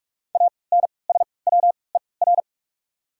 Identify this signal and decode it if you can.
Morse code, heard as a keyed tone, about 43 wpm, spaced as for 25 wpm, 690 Hz. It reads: ANSWER
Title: 3x Character Spacing - Top 300 Words - 45wpm